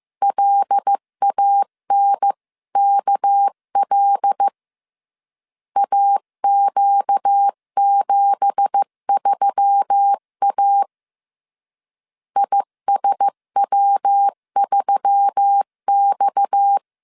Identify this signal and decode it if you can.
Morse code, heard as a keyed tone, about 15 wpm, 800 Hz. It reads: LANKL AQ73A ISW3X
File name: CW / DET / 1 9 1500C S B 14wpm 3302